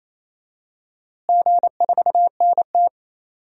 Morse code, heard as a keyed tone, about 28 wpm, 700 Hz. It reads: G4NT
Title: DXCC-100 Call Signs - 4 Characters - Repeated 2x - 28wpm